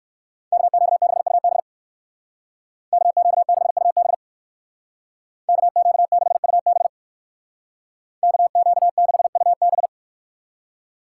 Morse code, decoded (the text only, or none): KQ6UB KQ6UB KQ6UB KQ6UB